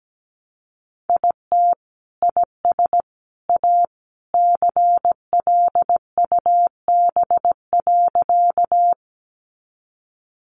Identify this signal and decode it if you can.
Morse code, heard as a keyed tone, about 17 wpm, 700 Hz. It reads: IT IS A CLUB.
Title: Sentences from Top 1200 Words 17wpm